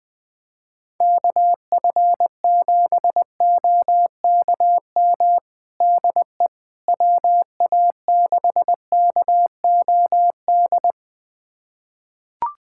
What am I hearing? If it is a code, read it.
Morse code, heard as a keyed tone, about 21 wpm, 700 Hz. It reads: KF7OKM DE WA6KOD
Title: USA Callsign DE Callsign - 6 Characters - 20wpm